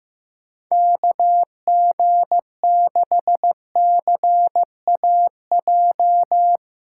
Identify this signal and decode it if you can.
Morse code, heard as a keyed tone, about 15 wpm, 700 Hz. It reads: KG6CAJ
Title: USA Callsign DE Callsign - 6 Characters - 15wpm